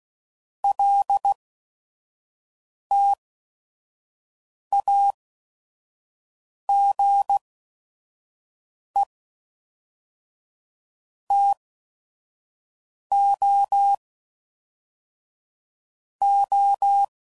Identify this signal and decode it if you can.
Morse code, heard as a keyed tone, about 16 wpm, spaced as for 5 wpm, 800 Hz. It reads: LTAGE TOO
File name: CW Code Groups @5WPM / Technical Subjects / XC